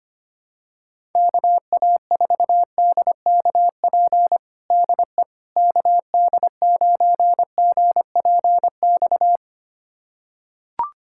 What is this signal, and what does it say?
Morse code, heard as a keyed tone, about 25 wpm, 700 Hz. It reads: KA4DKP DE KD9GPX
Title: USA Callsign DE Callsign 25wpm